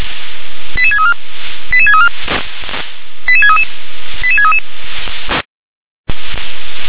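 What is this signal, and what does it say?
Five-tone selective call (ZVEI), heard as quick runs of five tones, tones of about 70 ms. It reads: 88634 88634 88633 88633
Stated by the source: Five-tone ZVEI selective call 88740, received by SDR.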